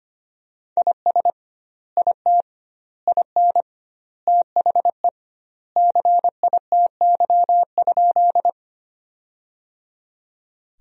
Morse code, heard as a keyed tone, about 25 wpm, 700 Hz. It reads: IS IT IN THE CITY?
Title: 2x Word-Spacing - Sentences from Top 300 Words - 25wpm